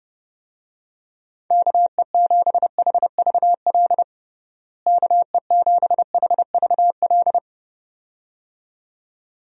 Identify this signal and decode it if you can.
Morse code, heard as a keyed tone, about 31 wpm, 700 Hz. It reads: KE7HVL KE7HVL
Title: USA Call Signs - 6 Characters - Repeated 2x - 30wpm